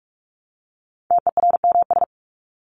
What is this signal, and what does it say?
Morse code, heard as a keyed tone, about 44 wpm, 700 Hz. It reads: TERMS
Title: ES:S Suffix - Top 100 Words with Base Words 45wpm